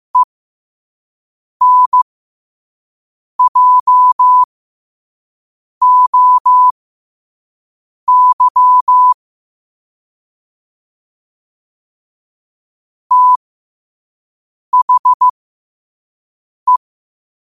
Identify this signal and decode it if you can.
Morse code, heard as a keyed tone, about 15 wpm, spaced as for 5 wpm, 1000 Hz. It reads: ENJOY THE
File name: qotdfiles / 5wpm